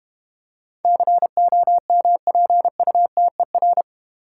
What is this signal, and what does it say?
Morse code, heard as a keyed tone, about 31 wpm, 690 Hz. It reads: COMPUTER
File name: Easy Words - Speed-Racing with Hints - 25wpm